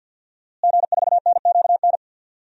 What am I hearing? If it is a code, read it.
Morse code, heard as a keyed tone, about 52 wpm, 700 Hz. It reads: G4NQN
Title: DXCC-100 Call Signs - 5 Characters - Repeated 3x - 50wpm